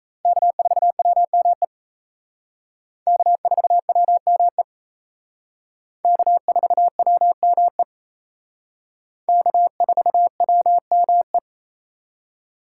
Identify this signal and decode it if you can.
Morse code, heard as a keyed tone, about 35 wpm, 700 Hz. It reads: K4WME K4WME K4WME K4WME
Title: Speed Racing - USA Call Signs 28wpm